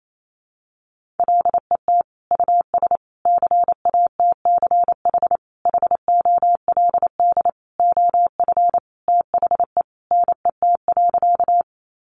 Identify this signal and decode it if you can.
Morse code, heard as a keyed tone, about 28 wpm, 700 Hz. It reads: LET US CATCH HOLD OF THE NET.
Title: Sentences from Top 1500 Words 28wpm